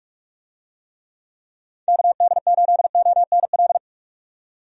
Morse code, heard as a keyed tone, about 46 wpm, 700 Hz. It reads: KD8ONL
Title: USA Call Signs - 6 Characters - 45wpm